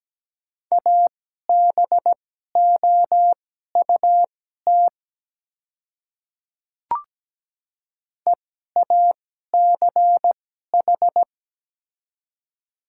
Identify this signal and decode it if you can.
Morse code, heard as a keyed tone, about 17 wpm, spaced as for 12 wpm, 700 Hz. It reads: ABOUT EACH
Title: 2x Character Spacing - Top 100 Words - 17wpm